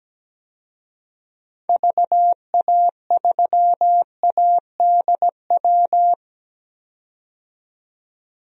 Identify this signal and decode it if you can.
Morse code, heard as a keyed tone, about 17 wpm, 700 Hz. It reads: VA3ADW
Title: DXCC-100 Call Signs - 6 Characters - 17wpm